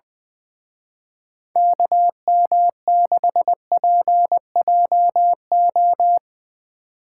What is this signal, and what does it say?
Morse code, heard as a keyed tone, about 22 wpm, 700 Hz. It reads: KM6PJO
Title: Speed Racing - USA Call Signs 20wpm